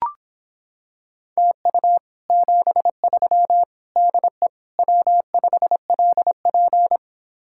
Morse code, heard as a keyed tone, about 26 wpm, 700 Hz. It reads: TU 73 DE W5LP